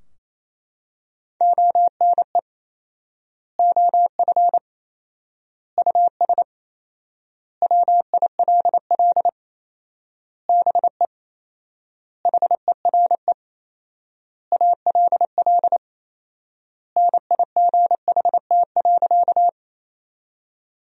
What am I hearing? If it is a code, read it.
Morse code, heard as a keyed tone, about 28 wpm, 700 Hz. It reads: ONE OF US WILL BE HERE ALL NIGHT.
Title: 4x Word-Spacing - Sentences from Top 300 Words - 28wpm